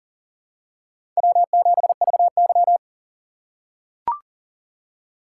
Morse code, heard as a keyed tone, about 40 wpm, 700 Hz. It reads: W7VY
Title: USA Call Signs - 3x Character Spacing - 40wpm